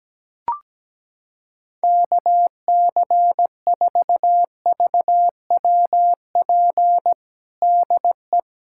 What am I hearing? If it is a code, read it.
Morse code, heard as a keyed tone, about 17 wpm, 700 Hz. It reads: KC4VWP DE
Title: USA Callsign DE Callsign 17wpm